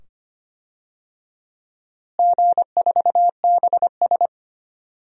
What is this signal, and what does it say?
Morse code, heard as a keyed tone, about 25 wpm, 700 Hz. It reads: G4BS
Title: DXCC-100 Call Signs - 4 Characters - Repeated 2x - 25wpm